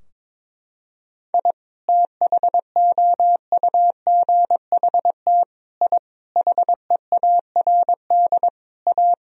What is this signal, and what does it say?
Morse code, heard as a keyed tone, about 22 wpm, 700 Hz. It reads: I THOUGHT I HEARD A
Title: Sentences from Top 1000 Words - Review Part 2 - 22wpm